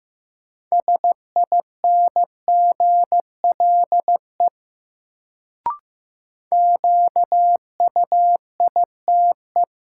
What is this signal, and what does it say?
Morse code, heard as a keyed tone, about 15 wpm, 700 Hz. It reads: SINGLE QUITE
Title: Sentences from Top 700 Words - New Word Drill - 15wpm